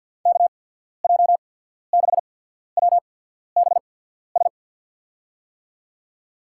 Morse code, heard as a keyed tone, about 53 wpm, spaced as for 14 wpm, 690 Hz. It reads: KJ6WBS
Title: USA Call Signs - 6 Characters - 8x Character Spacing - 50wpm